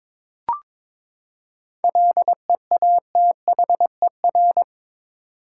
Morse code, heard as a keyed tone, about 22 wpm, 700 Hz. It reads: LEATHER
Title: Seven Letter Words - 22wpm